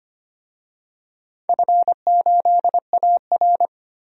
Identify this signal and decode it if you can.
Morse code, heard as a keyed tone, about 25 wpm, 700 Hz. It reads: F8AR
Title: DXCC-100 Call Signs - 4 Characters - 25wpm